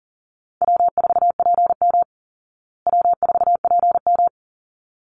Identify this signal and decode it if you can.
Morse code, heard as a keyed tone, about 40 wpm, 700 Hz. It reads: W4PM W4PM
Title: Call Signs Repeated 3x - 40wpm